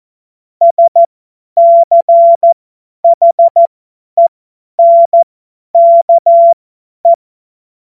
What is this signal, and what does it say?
Morse code, heard as a keyed tone, about 14 wpm, spaced as for 10 wpm, 680 Hz. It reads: SCHENKE
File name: Morsekurs / Witze / Witz 18. Juli 2023 50 F70